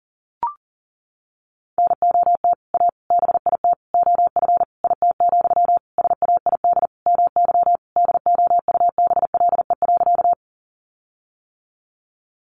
Morse code, heard as a keyed tone, about 40 wpm, 700 Hz. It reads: NOT A BIT OF IT, SAID MY DOUBLE.
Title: Sentences from Top 1400 Words 40wpm